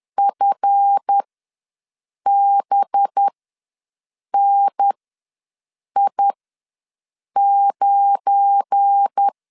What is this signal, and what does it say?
Morse code, heard as a keyed tone, about 10 wpm, spaced as for 6 wpm, 800 Hz. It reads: FBNI9
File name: LEZIONI / 7 GF9Z / 7 GF9 ws 10wpm 4034